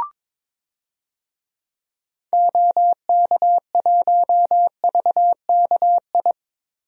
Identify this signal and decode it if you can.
Morse code, heard as a keyed tone, about 21 wpm, 700 Hz. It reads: OK1VKI
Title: DXCC-100 Call Signs - 6 Characters - Repeated 2x - 22wpm